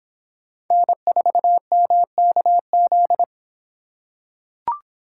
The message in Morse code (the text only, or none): N4MKZ